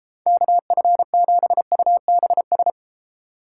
Morse code, heard as a keyed tone, about 37 wpm, 700 Hz. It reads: KF7UBS